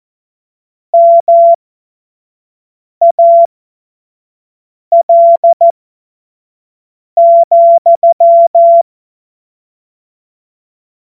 Morse code, heard as a keyed tone, about 14 wpm, spaced as for 5 wpm, 680 Hz. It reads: MAL,
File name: Morsekurs / Witze / Witz 28. März 2023 25 F70